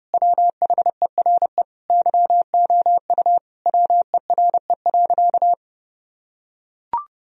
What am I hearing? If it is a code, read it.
Morse code, heard as a keyed tone, about 28 wpm, 700 Hz. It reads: WHERE YOU WERE.
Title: Sentences from Top 1000 Words - Review Part 4 - 30wpm